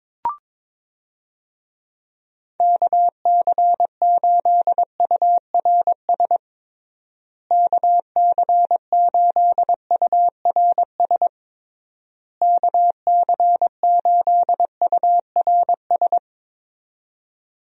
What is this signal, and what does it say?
Morse code, heard as a keyed tone, about 22 wpm, 700 Hz. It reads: KC8URS KC8URS KC8URS
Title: USA Call Signs - 6 Characters - Repeated 3x - 22wpm